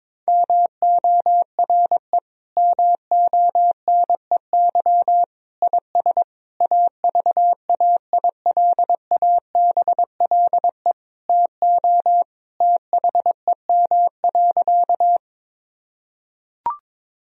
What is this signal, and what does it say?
Morse code, heard as a keyed tone, about 22 wpm, 700 Hz. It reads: MORE MONEY IS AVAILABLE TO THEM.
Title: Sentences from Top 1000 Words - Review Part 3 - 22wpm